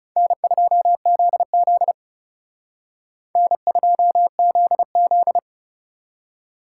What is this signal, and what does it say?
Morse code, heard as a keyed tone, about 34 wpm, 700 Hz. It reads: N2ZZ N2ZZ
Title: Speed Racing - USA 1x2 Call Signs 30wpm